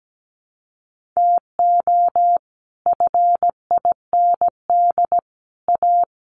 Morse code, heard as a keyed tone, about 17 wpm, 700 Hz. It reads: TO FIND A
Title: Sets of 5 Words 17wpm